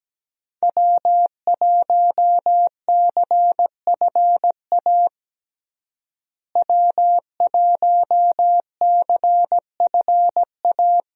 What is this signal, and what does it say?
Morse code, heard as a keyed tone, about 17 wpm, 700 Hz. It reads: W1CFA W1CFA
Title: USA Call Signs - 5 Characters - Repeated 3x - 17wpm